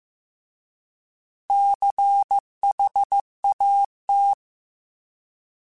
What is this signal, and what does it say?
Morse code, heard as a keyed tone, about 15 wpm, 800 Hz. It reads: CHAT